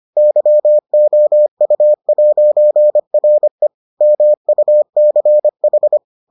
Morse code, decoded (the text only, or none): YOU'RE MUCH